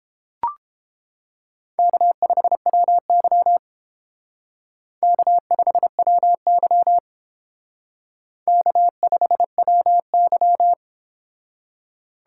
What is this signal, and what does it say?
Morse code, heard as a keyed tone, about 28 wpm, 700 Hz. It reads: K5WY K5WY K5WY